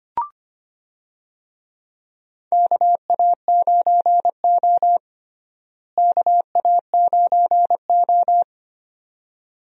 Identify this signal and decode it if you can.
Morse code, heard as a keyed tone, about 25 wpm, 700 Hz. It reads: KA9O KA9O
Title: USA Call Signs - 4 Characters - Repeated 2x - 25wpm